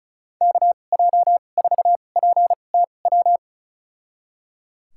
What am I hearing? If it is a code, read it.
Morse code, heard as a keyed tone, about 34 wpm, spaced as for 25 wpm, 700 Hz. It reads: KJ4PTW
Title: USA Call Signs - 6 Characters - 2x Character Spacing - 35wpm